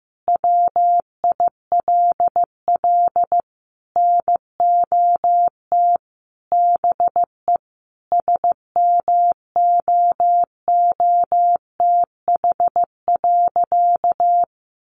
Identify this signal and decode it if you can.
Morse code, heard as a keyed tone, about 15 wpm, 700 Hz. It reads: WILL NOT BE SMOOTH.
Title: Sentences from Top 1400 Words 15wpm